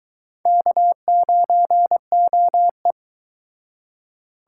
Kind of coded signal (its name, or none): Morse code